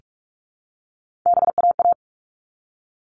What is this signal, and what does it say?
Morse code, heard as a keyed tone, about 44 wpm, 700 Hz. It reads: BAA
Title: All Cut Numbers - 3-Digit 45wpm